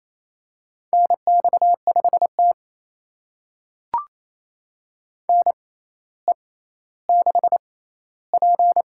Morse code, heard as a keyed tone, about 27 wpm, spaced as for 14 wpm, 700 Hz. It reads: NX5T NE6P
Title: USA Call Signs - 4 Characters - 6x Character Spacing - 28wpm